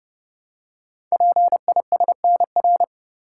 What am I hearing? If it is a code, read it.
Morse code, heard as a keyed tone, about 29 wpm, 700 Hz. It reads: PISNR
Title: Code Groups - 5 Letters - 30wpm